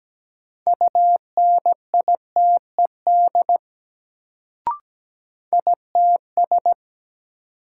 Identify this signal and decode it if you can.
Morse code, heard as a keyed tone, about 17 wpm, 700 Hz. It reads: UNITED ITS
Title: Sentences from Top 600 Words - New Word Drill - 17wpm